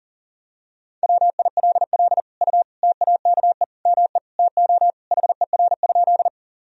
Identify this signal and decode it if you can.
Morse code, heard as a keyed tone, about 40 wpm, 700 Hz. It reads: WILL U TAKE ME TO HER?
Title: Sentences from Top 200 Words and Abbreviations 40wpm